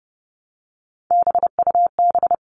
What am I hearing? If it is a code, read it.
Morse code, heard as a keyed tone, about 30 wpm, 700 Hz. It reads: BUB
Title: All Cut Numbers - 3-Digit 30wpm